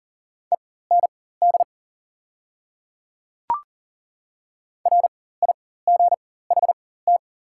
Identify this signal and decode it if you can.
Morse code, heard as a keyed tone, about 40 wpm, spaced as for 19 wpm, 690 Hz. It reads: END RIGHT
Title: 4x Character Spacing - Top 200 Words - 40wpm